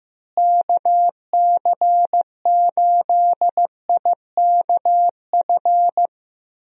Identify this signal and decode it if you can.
Morse code, heard as a keyed tone, about 15 wpm, 700 Hz. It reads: KC8IKF
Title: USA Call Signs - 6 Characters - Repeated 3x - 15wpm